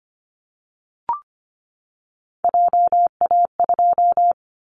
Morse code, heard as a keyed tone, about 25 wpm, 700 Hz. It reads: JA2
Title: Callsign Prefixes for Top 50 DXCC - 25wpm